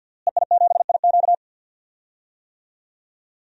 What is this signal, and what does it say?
Morse code, heard as a keyed tone, about 52 wpm, 700 Hz. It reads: EI7IQ
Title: DXCC-100 Call Signs - 5 Characters - Repeated 3x - 50wpm